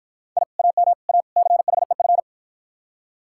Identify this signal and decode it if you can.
Morse code, heard as a keyed tone, about 52 wpm, 700 Hz. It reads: I AM A CHEF